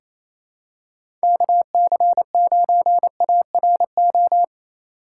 Morse code, heard as a keyed tone, about 28 wpm, 700 Hz. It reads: KC9ARO
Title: USA Call Signs - 4x Character Spacing - 28wpm